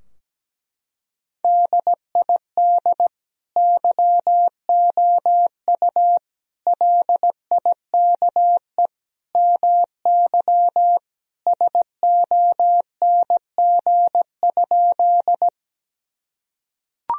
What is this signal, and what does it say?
Morse code, heard as a keyed tone, about 17 wpm, 700 Hz. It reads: DID YOU LIKE MY SONG?
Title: Sentences from Top 1000 Words - Review Part 2 - 17wpm